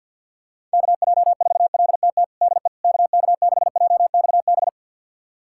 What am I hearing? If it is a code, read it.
Morse code, heard as a keyed tone, about 53 wpm, 700 Hz. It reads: KJ4LTT DE KK6JXB
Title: USA Callsign DE Callsign 50wpm